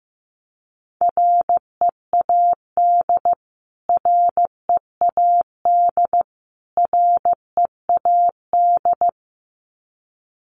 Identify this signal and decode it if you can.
Morse code, heard as a keyed tone, about 15 wpm, 700 Hz. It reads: READ READ READ
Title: Top 500 Words Repeated 15wpm